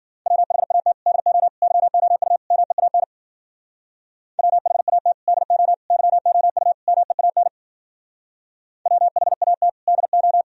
Morse code, heard as a keyed tone, about 50 wpm, 700 Hz. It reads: WHAT DO YOU MEAN WHAT DO YOU MEAN WHAT DO